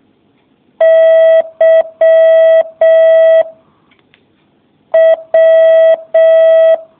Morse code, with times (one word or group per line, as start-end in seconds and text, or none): 0.8-3.4 Y
4.9-6.7 W